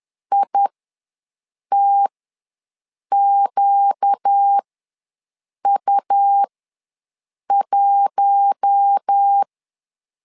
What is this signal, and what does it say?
Morse code, heard as a keyed tone, about 10 wpm, spaced as for 6 wpm, 800 Hz. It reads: ITQU1
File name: LEZIONI / 4 Q1 ws 10wpm 2538